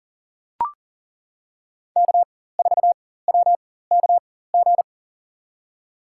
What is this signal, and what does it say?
Morse code, heard as a keyed tone, about 40 wpm, spaced as for 19 wpm, 700 Hz. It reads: K4WKG
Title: USA Call Signs - 5 Characters - 4x Character Spacing - 40wpm